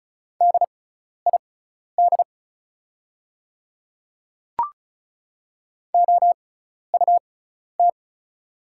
Morse code, heard as a keyed tone, about 36 wpm, spaced as for 12 wpm, 700 Hz. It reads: DID OUT
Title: 6x Character Spacing - Top 200 Words - 35wpm